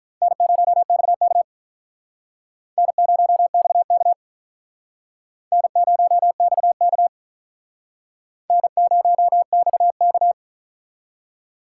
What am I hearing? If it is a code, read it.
Morse code, heard as a keyed tone, about 45 wpm, 700 Hz. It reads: N0XK N0XK N0XK N0XK